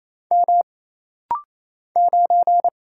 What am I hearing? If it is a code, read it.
Morse code, heard as a keyed tone, about 27 wpm, 700 Hz. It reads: M 9
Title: Single Letter-Number - Rapid-Fire 28wpm